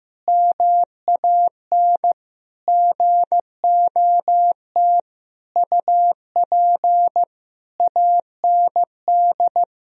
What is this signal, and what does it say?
Morse code, heard as a keyed tone, about 15 wpm, 700 Hz. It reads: MAN GOT UP AND
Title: Sentences from Top 1000 Words - Review Part 1 - 15wpm